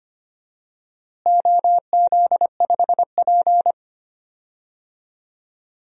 Morse code, heard as a keyed tone, about 25 wpm, 700 Hz. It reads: OZ5P
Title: DXCC-100 Call Signs - 4 Characters - Repeated 3x - 25wpm